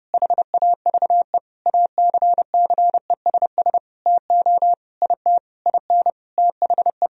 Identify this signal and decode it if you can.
Morse code, heard as a keyed tone, about 32 wpm, 700 Hz. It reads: HAVE ACCESS TO IT IN THE
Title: Sentences from Top 1000 Words - Review Part 4 - 30wpm